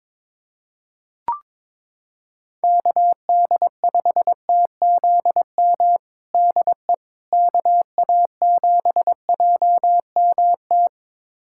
Morse code, heard as a keyed tone, about 22 wpm, 700 Hz. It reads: KD5TZM DE KA7JMT
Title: USA Callsign DE Callsign 22wpm